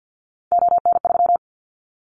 Morse code, heard as a keyed tone, about 50 wpm, 700 Hz. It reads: ON3
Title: Callsign Prefixes for Top 50 DXCC - 50wpm